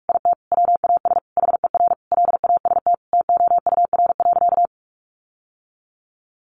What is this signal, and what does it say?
Morse code, heard as a keyed tone, about 44 wpm, 700 Hz. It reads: IT WAS HER LAST TOUR.
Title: Sentences from Top 1400 Words 45wpm